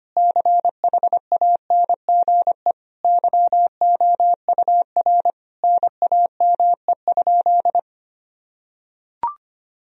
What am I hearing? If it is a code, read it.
Morse code, heard as a keyed tone, about 25 wpm, 700 Hz. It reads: CHANGE YOUR NAME?
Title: Sentences from Top 1000 Words - Review Part 1 - 25wpm